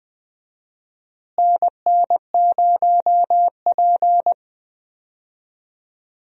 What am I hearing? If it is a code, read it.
Morse code, heard as a keyed tone, about 21 wpm, 700 Hz. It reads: NN0P
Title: USA Call Signs - 4 Characters - Repeated 3x - 20wpm